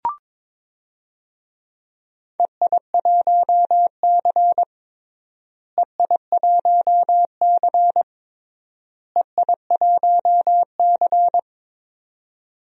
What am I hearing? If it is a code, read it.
Morse code, heard as a keyed tone, about 21 wpm, 700 Hz. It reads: EI1C EI1C EI1C
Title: DXCC-100 Call Signs - 4 Characters - Repeated 3x - 22wpm